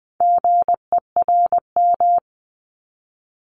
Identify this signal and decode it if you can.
Morse code, heard as a keyed tone, about 20 wpm, 700 Hz. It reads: GERM
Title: Multi-letter Phonemes Repeated 3x 20wpm